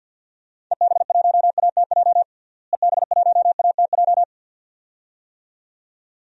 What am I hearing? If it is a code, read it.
Morse code, heard as a keyed tone, about 47 wpm, 700 Hz. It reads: EB1ATJ EB1ATJ
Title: DXCC-100 Call Signs - 6 Characters - Repeated 2x - 50wpm